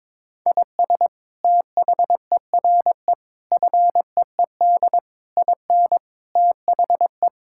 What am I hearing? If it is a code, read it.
Morse code, heard as a keyed tone, about 21 wpm, 700 Hz. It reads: IS THERE FEED IN THE
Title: Sentences from Top 1600 Words 22wpm